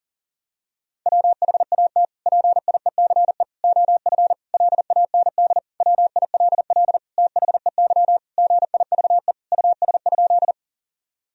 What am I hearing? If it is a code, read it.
Morse code, heard as a keyed tone, about 40 wpm, 700 Hz. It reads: WHAT PIECE OF LAND WILL THEY GIVE US?
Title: Sentences from Top 1000 Words - Review Part 2 - 40wpm